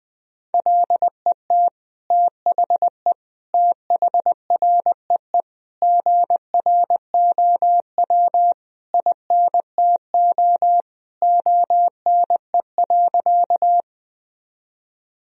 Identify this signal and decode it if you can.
Morse code, heard as a keyed tone, about 20 wpm, 700 Hz. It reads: LET THE THREE GROW INTO ONE.
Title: Sentences from Top 1000 Words - Review Part 1 - 20wpm